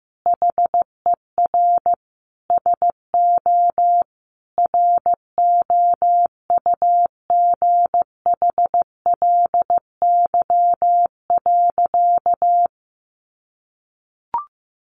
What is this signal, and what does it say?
Morse code, heard as a keyed tone, about 15 wpm, 700 Hz. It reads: HER SO ROUGHLY.
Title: Sentences from Top 1500 Words 15wpm